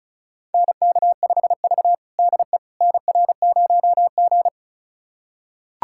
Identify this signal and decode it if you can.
Morse code, heard as a keyed tone, about 34 wpm, 700 Hz. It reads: NK5V DE NR0G
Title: USA Callsign DE Callsign - 4 Characters - 35wpm